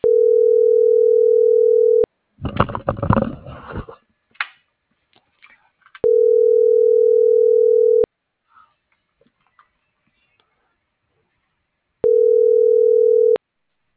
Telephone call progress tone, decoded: ringback tone

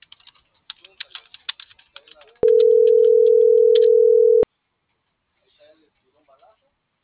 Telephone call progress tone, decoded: ringback tone

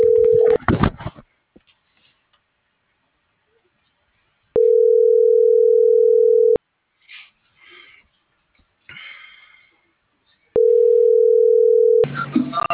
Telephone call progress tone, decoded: ringback tone

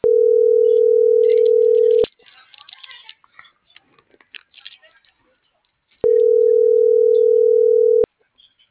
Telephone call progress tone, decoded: ringback tone

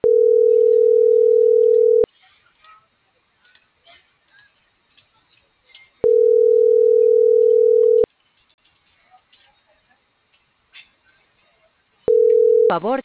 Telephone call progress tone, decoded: ringback tone

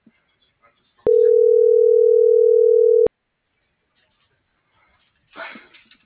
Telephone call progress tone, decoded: ringback tone

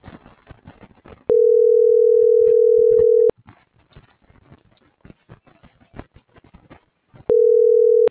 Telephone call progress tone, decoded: ringback tone

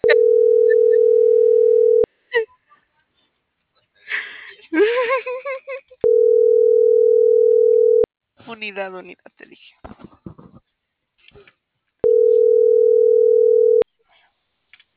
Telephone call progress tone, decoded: ringback tone

